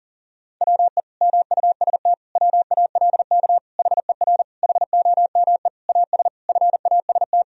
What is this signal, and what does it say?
Morse code, heard as a keyed tone, about 40 wpm, 700 Hz. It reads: WE MUST WALK HER HOME AS FAST
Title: Sentences from Top 1000 Words - Review Part 2 - 40wpm